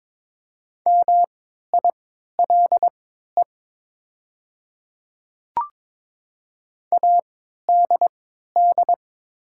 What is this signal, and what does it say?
Morse code, heard as a keyed tone, about 22 wpm, spaced as for 12 wpm, 700 Hz. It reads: MILE ADD
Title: 3x Character Spacing - Top 300 Words - 22wpm